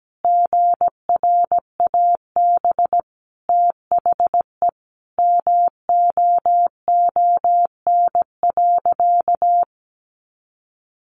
Morse code, heard as a keyed tone, about 17 wpm, 700 Hz. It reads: GRAB THE MOON.
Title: A Sound Of Thunder 17wpm